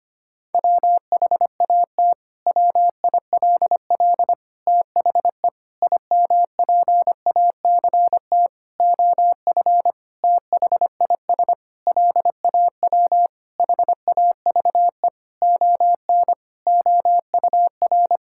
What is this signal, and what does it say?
Morse code, heard as a keyed tone, about 25 wpm, 700 Hz. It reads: WHAT WILL THE IMPACT OF THIS LAW HAVE ON OUR
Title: Sentences from Top 1000 Words - Review Part 6 - 25wpm